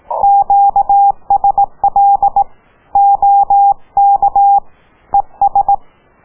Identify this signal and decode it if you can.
Morse code, heard as a keyed tone, about 18 wpm, 800 Hz. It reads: QSL OK ES